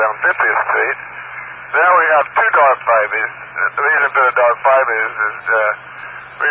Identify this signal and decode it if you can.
Emergency Alert System (SAME) tones (header bursts), heard as harsh buzzing bursts, with speech. It is heard from 1.0 to 5.5 s.